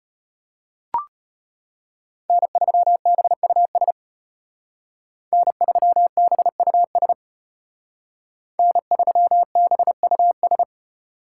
Morse code, heard as a keyed tone, about 33 wpm, 700 Hz. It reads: N3BUS N3BUS N3BUS